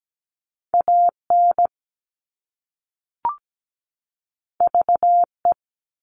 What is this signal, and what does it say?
Morse code, heard as a keyed tone, about 17 wpm, 700 Hz. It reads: AN VE
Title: All Cut Numbers - 2-Digit 17wpm